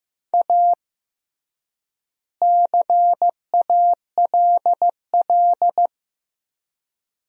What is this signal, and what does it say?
Morse code, heard as a keyed tone, about 15 wpm, 700 Hz. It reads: A CALL
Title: 3x Word-Spacing - Sentences from Top 300 Words - 15wpm